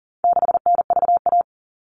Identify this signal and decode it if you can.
Morse code, heard as a keyed tone, about 40 wpm, 700 Hz. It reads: 6NVA
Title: All Cut Numbers - 4-Digit 40wpm